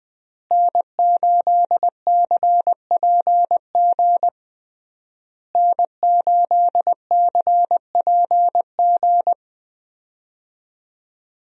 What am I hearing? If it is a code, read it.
Morse code, heard as a keyed tone, about 20 wpm, 700 Hz. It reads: N8CPG N8CPG
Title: USA Call Signs - 5 Characters - Repeated 3x - 20wpm